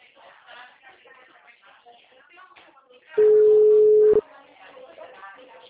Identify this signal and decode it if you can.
Ringback tone (Europe, 425 Hz), with speech.